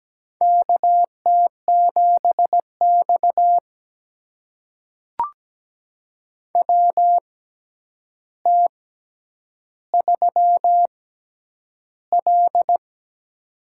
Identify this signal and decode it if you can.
Morse code, heard as a keyed tone, about 17 wpm, spaced as for 9 wpm, 700 Hz. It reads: KT7X WT3L
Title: USA Call Signs - 4 Characters - 6x Character Spacing - 17wpm